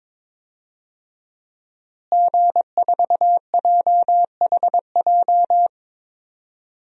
Morse code, heard as a keyed tone, about 22 wpm, 700 Hz. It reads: G4JHJ
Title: DXCC-100 Call Signs - 5 Characters - Repeated 3x - 22wpm